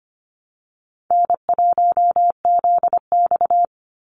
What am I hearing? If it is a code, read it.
Morse code, heard as a keyed tone, about 25 wpm, 700 Hz. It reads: N1ZX
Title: Easy Call Signs Repeated 3x - 25wpm